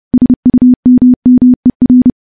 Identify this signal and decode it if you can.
Morse code, heard as a keyed tone, about 32 wpm, 250 Hz. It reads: SUMMER
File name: morse / morse-training-30WPM